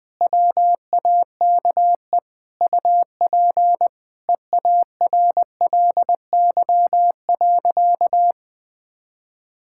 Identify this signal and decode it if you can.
Morse code, heard as a keyed tone, about 20 wpm, 700 Hz. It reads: WAKE UP EARLY.
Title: Sentences with 5-Word Phrases - 20wpm